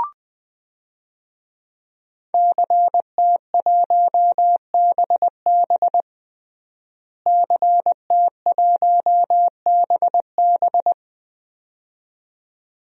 Morse code, heard as a keyed tone, about 20 wpm, 700 Hz. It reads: CT1BB CT1BB
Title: DXCC-100 Call Signs - 5 Characters - Repeated 2x - 20wpm